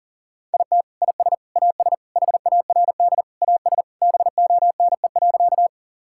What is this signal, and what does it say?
Morse code, heard as a keyed tone, about 37 wpm, 700 Hz. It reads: IT IS AS HARD AS BONE.
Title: Sentences from Top 1600 Words 40wpm